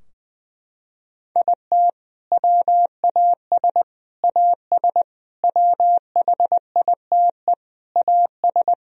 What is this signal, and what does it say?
Morse code, heard as a keyed tone, about 21 wpm, 700 Hz. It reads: IT WAS AS WHITE AS
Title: Sentences from Top 1000 Words - Review Part 3 - 20wpm